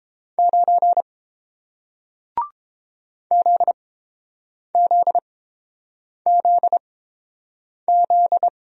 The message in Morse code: 9 ZZZZ

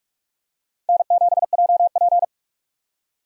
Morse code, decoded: N7JP

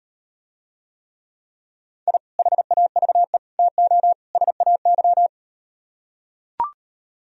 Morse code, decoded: I HAVE TO SAY

